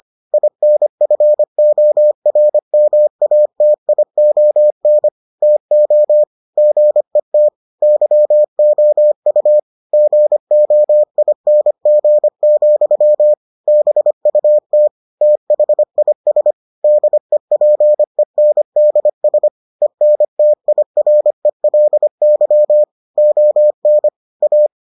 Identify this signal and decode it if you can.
Morse code, heard as a keyed tone, about 25 wpm, 600 Hz. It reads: INFORMATION TO GET YOU GOING, BUT THIS DEPENDS ENTIRELY ON A